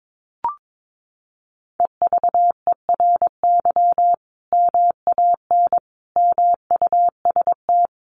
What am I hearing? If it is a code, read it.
Morse code, heard as a keyed tone, about 22 wpm, 700 Hz. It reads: EVERY MAN MUST
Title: Sentences from Top 1400 Words 22wpm